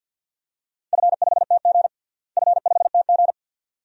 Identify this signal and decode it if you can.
Morse code, heard as a keyed tone, about 51 wpm, 700 Hz. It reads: F5TG F5TG
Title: DXCC-100 Call Signs - 4 Characters - Repeated 2x - 50wpm